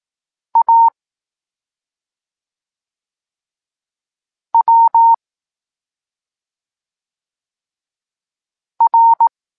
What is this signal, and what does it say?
Morse code, heard as a keyed tone, about 18 wpm, 920 Hz. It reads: A W R
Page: Tekst 23.3 – Ciąg dalszy doskonalenia odbioru poznanych liter